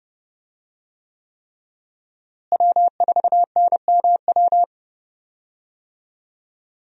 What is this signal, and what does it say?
Morse code, heard as a keyed tone, about 30 wpm, 700 Hz. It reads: W4NMW
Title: USA Call Signs - 5 Characters - Repeated 3x - 30wpm